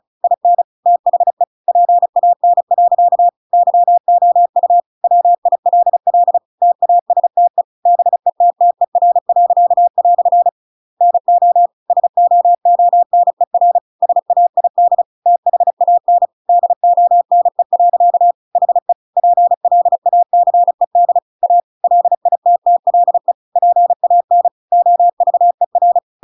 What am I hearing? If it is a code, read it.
Morse code, heard as a keyed tone, about 34 wpm, 700 Hz. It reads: IN THE PAN. YOU WILL TASTE BETTER." NO SOONER SAID THAN DONE. HE PLACED A LITTLE PAN OVER